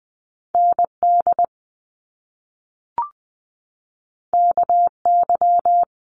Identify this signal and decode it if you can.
Morse code, heard as a keyed tone, about 20 wpm, 700 Hz. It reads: ND KY
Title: US States 20wpm